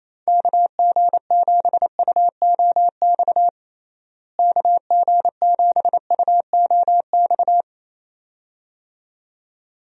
Morse code, heard as a keyed tone, about 28 wpm, 700 Hz. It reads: KG7UOX KG7UOX
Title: USA Call Signs - 6 Characters - Repeated 2x - 28wpm